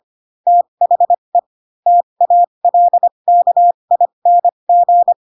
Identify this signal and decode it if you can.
Morse code, heard as a keyed tone, about 25 wpm, spaced as for 22 wpm, 700 Hz. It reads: THE TALKING
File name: Pinocchio / Pinocchio ch4 18 25 ik1hge